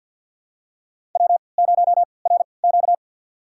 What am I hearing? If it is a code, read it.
Morse code, heard as a keyed tone, about 51 wpm, spaced as for 28 wpm, 700 Hz. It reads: W0RQ
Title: USA Call Signs - 4 Characters - 3x Character Spacing - 50wpm